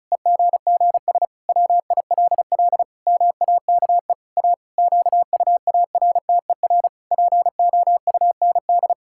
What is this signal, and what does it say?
Morse code, heard as a keyed tone, about 36 wpm, 700 Hz. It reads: EGGS WILL MAKE A QUARTER POUND
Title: Pangram Challenge v2 - 35wpm